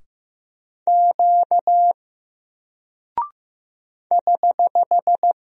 Morse code, heard as a keyed tone, about 15 wpm, 700 Hz. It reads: Q <HH>
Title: Single Character (Letter:Number:Punctuation:Prosign) Alt 15wpm